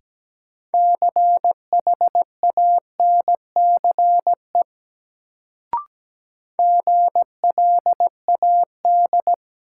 Morse code, heard as a keyed tone, about 17 wpm, 700 Hz. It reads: CHANCE GLAD
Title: Sentences from Top 800 Words - New Word Drill - 17wpm